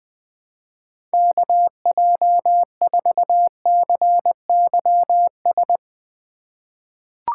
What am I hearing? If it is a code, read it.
Morse code, heard as a keyed tone, about 21 wpm, 700 Hz. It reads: KJ4CYS
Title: USA Call Signs - 3x Character Spacing - 20wpm